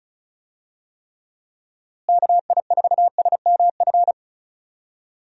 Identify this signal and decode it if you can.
Morse code, heard as a keyed tone, about 34 wpm, 700 Hz. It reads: KI4SMF
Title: USA Call Signs - 6 Characters - Repeated 3x - 35wpm